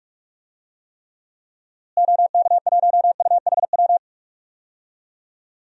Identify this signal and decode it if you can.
Morse code, heard as a keyed tone, about 43 wpm, 700 Hz. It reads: OK1UHW
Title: DXCC-100 Call Signs - 6 Characters - Repeated 3x - 45wpm